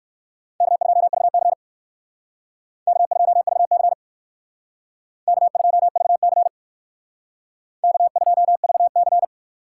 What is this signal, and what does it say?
Morse code, heard as a keyed tone, about 57 wpm, 700 Hz. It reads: K2VC K2VC K2VC K2VC